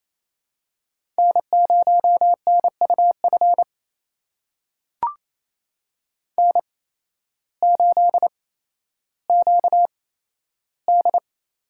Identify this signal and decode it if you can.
Morse code, heard as a keyed tone, about 28 wpm, spaced as for 13 wpm, 700 Hz. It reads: N0NUF N8QD
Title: USA Call Signs - 5 Characters - 8x Character Spacing - 28wpm